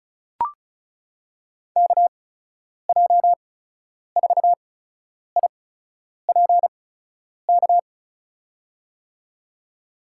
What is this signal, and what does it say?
Morse code, heard as a keyed tone, about 36 wpm, spaced as for 10 wpm, 700 Hz. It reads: KJ4IPK
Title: USA Call Signs - 6 Characters - 8x Character Spacing - 35wpm